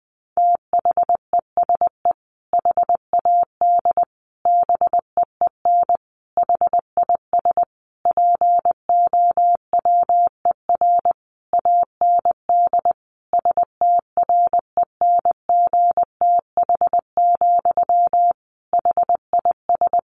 Morse code, heard as a keyed tone, about 20 wpm, 700 Hz. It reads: THESE HAD BEEN HIS POWER AND STRENGTH, HIS
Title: Sentences from Top 1500 Words 20wpm